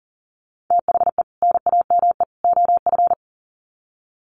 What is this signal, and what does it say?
Morse code, heard as a keyed tone, about 40 wpm, 700 Hz. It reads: THE NAME OF
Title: Sets of 3 Words 40wpm